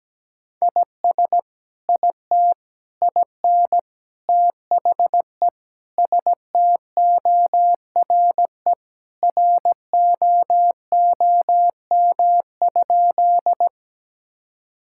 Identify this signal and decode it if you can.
Morse code, heard as a keyed tone, about 17 wpm, 700 Hz. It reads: IS IT IN THE STORE ROOM?
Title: Sentences from Top 1000 Words - Review Part 5 - 17wpm